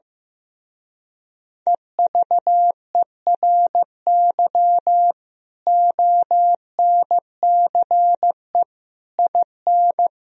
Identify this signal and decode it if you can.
Morse code, heard as a keyed tone, about 15 wpm, 700 Hz. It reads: EVERY ONCE IN
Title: Speed-Racing - Sets of 5 Words 15wpm